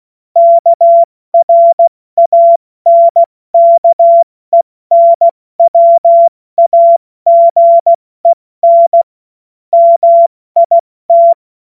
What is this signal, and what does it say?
Morse code, heard as a keyed tone, about 16 wpm, 680 Hz. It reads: KRANKENWAGEN MIT
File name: Morsekurs / Witze / Witz 30 Januar 2024 70 F80